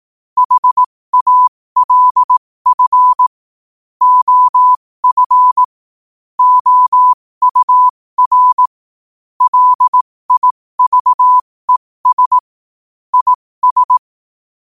Morse code, heard as a keyed tone, about 18 wpm, spaced as for 15 wpm, 1000 Hz. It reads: HALF OF OUR LIVES IS
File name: 15wpm